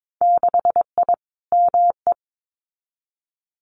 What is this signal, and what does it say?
Morse code, heard as a keyed tone, about 22 wpm, 700 Hz. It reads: 6I ME
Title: Winter Field Day Exchanges Encore 22wpm